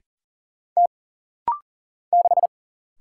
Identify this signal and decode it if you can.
Morse code, heard as a keyed tone, about 40 wpm, 700 Hz. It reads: T 6
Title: Single Letter-Number Alt - Rapid-Fire 40wpm